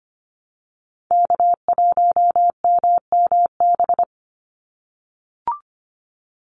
Morse code, heard as a keyed tone, about 25 wpm, 700 Hz. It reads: K1MMB
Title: Call Signs - 25wpm